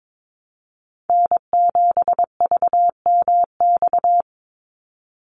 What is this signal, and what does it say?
Morse code, heard as a keyed tone, about 22 wpm, 700 Hz. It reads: N7VMX
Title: USA Call Signs 22wpm